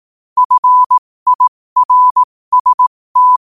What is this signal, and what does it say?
Morse code, heard as a keyed tone, about 18 wpm, spaced as for 15 wpm, 1000 Hz. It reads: FIRST
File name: qotdfiles / 15wpm